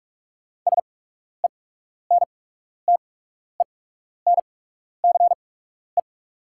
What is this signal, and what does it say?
Morse code, heard as a keyed tone, about 44 wpm, spaced as for 12 wpm, 690 Hz. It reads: SENTENCE
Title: 8x Character Spacing - Top 300 Words - 45wpm